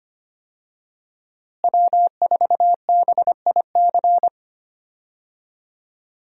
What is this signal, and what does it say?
Morse code, heard as a keyed tone, about 25 wpm, 700 Hz. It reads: W4BIC
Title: USA Call Signs - 5 Characters - Repeated 3x - 25wpm